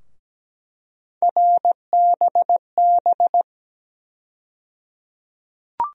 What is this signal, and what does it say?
Morse code, heard as a keyed tone, about 17 wpm, 700 Hz. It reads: RBB
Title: USA Call Sign Postfixes - 3 Letters 17wpm